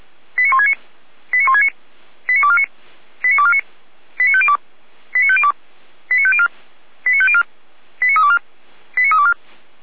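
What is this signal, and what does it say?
Five-tone selective call (ZVEI), heard as quick runs of five tones, tones of about 80 ms. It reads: 89169 89169 89249 89249 89502 89502 89504 89504 89324 89324
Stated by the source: ZVEI five-tone selective call received by SDR.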